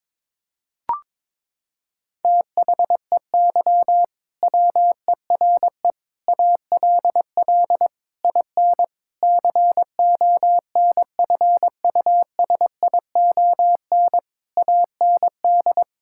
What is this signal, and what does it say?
Morse code, heard as a keyed tone, about 22 wpm, 700 Hz. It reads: THEY WERE ALL IN CONFUSION AND
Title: Sentences from Top 1600 Words 22wpm